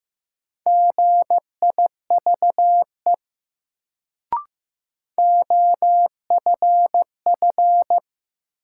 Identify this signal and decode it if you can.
Morse code, heard as a keyed tone, about 15 wpm, 700 Hz. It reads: GIVE OFF
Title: Sets of 2 Words - New Word Drill - 15wpm